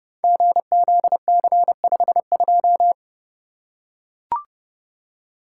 Morse code, heard as a keyed tone, about 30 wpm, 700 Hz. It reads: GZC52
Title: Code Groups - 5 Letter:Numbers - 30wpm